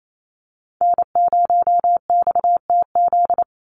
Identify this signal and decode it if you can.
Morse code, heard as a keyed tone, about 28 wpm, 700 Hz. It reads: N0XTZ
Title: USA Call Signs 28wpm